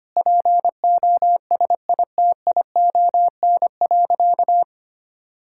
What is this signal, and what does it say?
Morse code, heard as a keyed tone, about 25 wpm, 700 Hz. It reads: POSITION.
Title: Sentences from Top 1000 Words - Review Part 5 - 25wpm